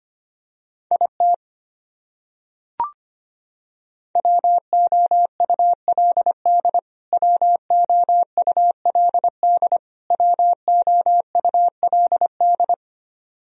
Morse code, heard as a keyed tone, about 25 wpm, 700 Hz. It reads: IT WOULD WOULD WOULD
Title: Top 100 Words Repeated 25wpm